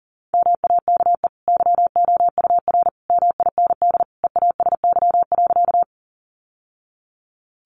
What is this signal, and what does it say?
Morse code, heard as a keyed tone, about 40 wpm, 700 Hz. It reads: MAKE YOUR MIND EASY.
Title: Sentences from Top 700 Words 40wpm